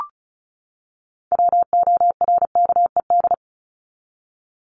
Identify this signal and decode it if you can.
Morse code, heard as a keyed tone, about 34 wpm, 700 Hz. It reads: WORKED
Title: ED Suffix - Top 100 Words with Base Words 35wpm